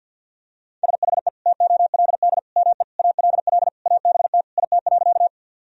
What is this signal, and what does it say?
Morse code, heard as a keyed tone, about 52 wpm, 700 Hz. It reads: SHE TOLD ME ALL ABT IT.
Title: Sentences from Top 400 Words and Abbreviations 50wpm